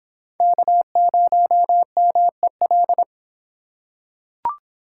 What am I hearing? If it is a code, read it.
Morse code, heard as a keyed tone, about 23 wpm, 700 Hz. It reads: K0MEL